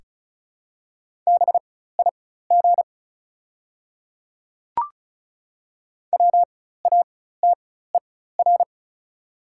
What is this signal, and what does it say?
Morse code, heard as a keyed tone, about 36 wpm, spaced as for 16 wpm, 690 Hz. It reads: BIG WATER
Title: 4x Character Spacing - Top 300 Words - 35wpm